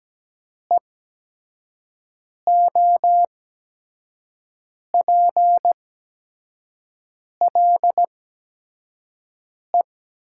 Morse code, heard as a keyed tone, about 17 wpm, spaced as for 5 wpm, 700 Hz. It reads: EOPLE
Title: 8x Character Spacing - Top 100 Words - 17wpm